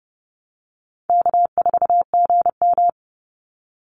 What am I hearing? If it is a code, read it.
Morse code, heard as a keyed tone, about 30 wpm, 700 Hz. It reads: K4GM
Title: Easy Call Signs Repeated 2x - 30wpm